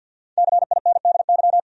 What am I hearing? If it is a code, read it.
Morse code, heard as a keyed tone, about 52 wpm, 700 Hz. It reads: CINDY
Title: USA Common Names - 50wpm